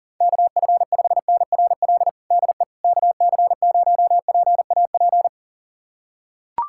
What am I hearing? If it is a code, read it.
Morse code, heard as a keyed tone, about 40 wpm, 700 Hz. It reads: KF5NRL DE KC0PAP
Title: USA Callsign DE Callsign 40wpm